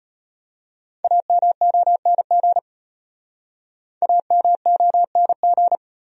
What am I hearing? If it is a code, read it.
Morse code, heard as a keyed tone, about 31 wpm, 700 Hz. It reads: AMONG AMONG
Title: Speed-Racing Top 500 Words 25wpm